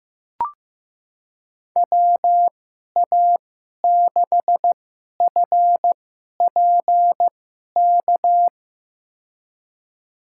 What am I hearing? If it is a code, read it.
Morse code, heard as a keyed tone, about 15 wpm, spaced as for 11 wpm, 700 Hz. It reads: WA6FPK
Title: USA Call Signs - 6 Characters - 2x Character Spacing - 15wpm